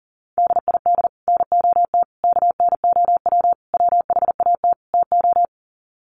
Morse code, heard as a keyed tone, about 40 wpm, 700 Hz. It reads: DID NOT KNOW WHAT TO
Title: Sets of 5 Words 40wpm